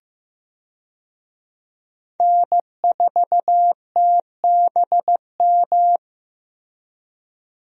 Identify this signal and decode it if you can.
Morse code, heard as a keyed tone, about 15 wpm, 700 Hz. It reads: N4TBM
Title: USA Call Signs - 5 Characters - Repeated 3x - 15wpm